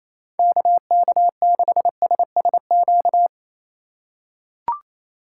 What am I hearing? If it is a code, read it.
Morse code, heard as a keyed tone, about 27 wpm, 700 Hz. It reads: KK6SSQ
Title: USA Call Signs - 6 Characters - 4x Character Spacing - 28wpm